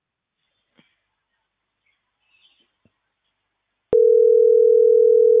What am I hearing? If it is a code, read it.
Ringback tone (North America), 440 + 480 Hz.